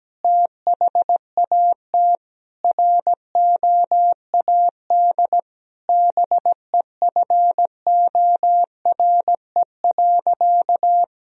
Morse code, heard as a keyed tone, about 17 wpm, 700 Hz. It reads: THAT ROAD BEFORE.
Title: Sentences from Top 1000 Words - Review Part 2 - 17wpm